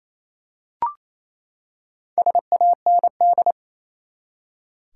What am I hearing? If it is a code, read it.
Morse code, heard as a keyed tone, about 28 wpm, 700 Hz. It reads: SAND
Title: Four Letter Words - 28wpm